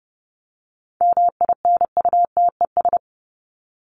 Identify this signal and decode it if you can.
Morse code, heard as a keyed tone, about 30 wpm, 700 Hz. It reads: MINUTES
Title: ES:S Suffix - Top 100 Words with Base Words 30wpm